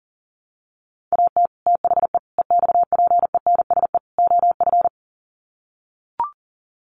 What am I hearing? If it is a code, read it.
Morse code, heard as a keyed tone, about 37 wpm, 700 Hz. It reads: AT THE EXPENSE OF
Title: Sets of 4 Words 40wpm